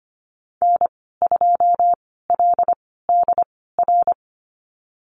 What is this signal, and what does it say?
Morse code, heard as a keyed tone, about 25 wpm, spaced as for 16 wpm, 700 Hz. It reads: N2LDR
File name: Lesson 15g - Number 5 - Call Signs 25wpm